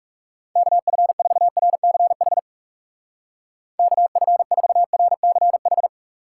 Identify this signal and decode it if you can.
Morse code, heard as a keyed tone, about 39 wpm, 700 Hz. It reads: KF4RCH KF4RCH